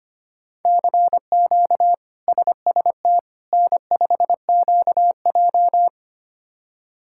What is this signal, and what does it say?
Morse code, heard as a keyed tone, about 25 wpm, 700 Hz. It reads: CQ SST N5QJ
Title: K1USN SST QSO Practice - 25wpm